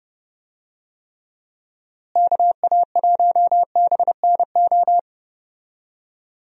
Morse code, heard as a keyed tone, about 29 wpm, 700 Hz. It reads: KA1BNO
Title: USA Call Signs - 6 Characters - Repeated 3x - 30wpm